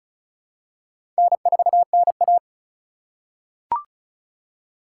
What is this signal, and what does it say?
Morse code, heard as a keyed tone, about 36 wpm, 700 Hz. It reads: N4NA
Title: USA Call Signs - 4 Characters - 3x Character Spacing - 35wpm